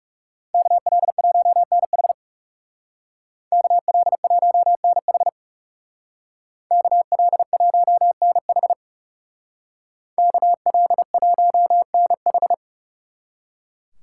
Morse code, heard as a keyed tone, about 37 wpm, 700 Hz. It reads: KL1NH KL1NH KL1NH KL1NH
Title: Speed Racing - USA 2x2 Call Signs 30wpm